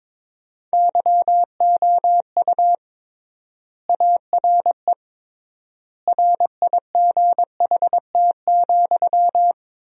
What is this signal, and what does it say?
Morse code, heard as a keyed tone, about 22 wpm, 700 Hz. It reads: YOU ARE RIGHT,
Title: 3x Word-Spacing - Sentences from Top 300 Words - 22wpm